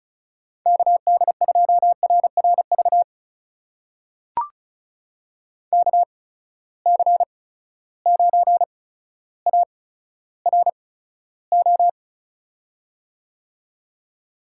Morse code, heard as a keyed tone, about 36 wpm, spaced as for 15 wpm, 700 Hz. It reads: KD2RRV KC9ARO
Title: USA Call Signs - 6 Characters - 8x Character Spacing - 35wpm